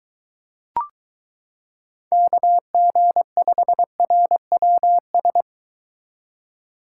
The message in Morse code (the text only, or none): KG5RWS